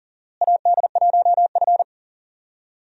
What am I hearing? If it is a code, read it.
Morse code, heard as a keyed tone, about 40 wpm, 700 Hz. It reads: AD1F
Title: USA Call Signs - 4 Characters - Repeated 2x - 40wpm